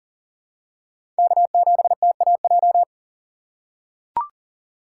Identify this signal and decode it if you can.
Morse code, heard as a keyed tone, about 40 wpm, 700 Hz. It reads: K7TAJ
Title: USA Call Signs - 5 Characters - 4x Character Spacing - 40wpm